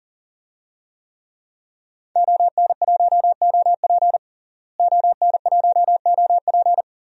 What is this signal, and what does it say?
Morse code, heard as a keyed tone, about 40 wpm, 700 Hz. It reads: ON1OP ON1OP
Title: DXCC-100 Call Signs - 5 Characters - Repeated 2x - 40wpm